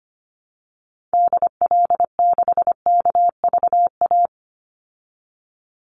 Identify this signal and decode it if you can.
Morse code, heard as a keyed tone, about 25 wpm, 700 Hz. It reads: DL6KVA
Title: Call Signs Repeated 2x - 25wpm